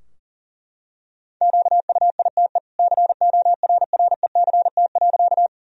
Morse code, heard as a keyed tone, about 40 wpm, 700 Hz. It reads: QUITE CORRECT.
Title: Sentences from Top 1000 Words - Review Part 2 - 40wpm